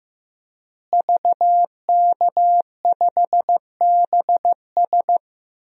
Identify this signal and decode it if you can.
Morse code, heard as a keyed tone, about 15 wpm, 700 Hz. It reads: VK5BS
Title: DXCC-100 Call Signs - 5 Characters - Repeated 2x - 15wpm